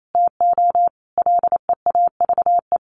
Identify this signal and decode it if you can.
Morse code, heard as a keyed tone, about 28 wpm, 700 Hz. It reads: TO LEAVE
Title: Sentences from Top 1100 Words 28wpm